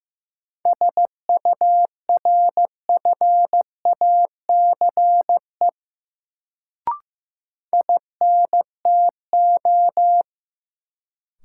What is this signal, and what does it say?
Morse code, heard as a keyed tone, about 15 wpm, 700 Hz. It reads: SURFACE INTO
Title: Sentences from Top 600 Words - New Word Drill - 15wpm